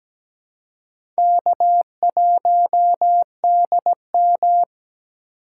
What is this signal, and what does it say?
Morse code, heard as a keyed tone, about 17 wpm, 700 Hz. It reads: K1DM
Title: K1USN SST QSO Practice - 17wpm